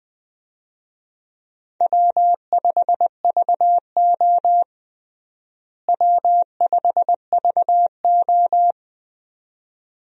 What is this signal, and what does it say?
Morse code, heard as a keyed tone, about 20 wpm, 700 Hz. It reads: W5VO W5VO
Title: USA Call Signs - 4 Characters - Repeated 2x - 20wpm